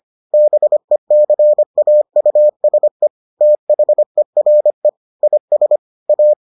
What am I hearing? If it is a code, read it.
Morse code, heard as a keyed tone, about 25 wpm, 600 Hz. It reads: BECAUSE THERE IS A